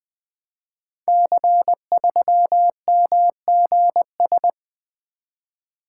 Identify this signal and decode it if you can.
Morse code, heard as a keyed tone, about 20 wpm, 700 Hz. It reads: C3MGS
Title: Code Groups - 5 Letter:Numbers - 20wpm